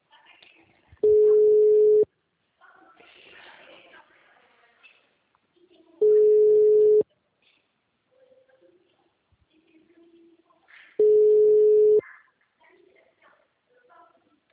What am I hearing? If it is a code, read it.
Ringback tone (Europe, 425 Hz).
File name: out- -2023- - - .87906